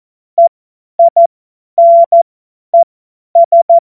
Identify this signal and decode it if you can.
Morse code, heard as a keyed tone, about 14 wpm, spaced as for 10 wpm, 680 Hz. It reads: EINES
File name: Morsekurs / Witze / Witz 17. Mai 2022 50 F70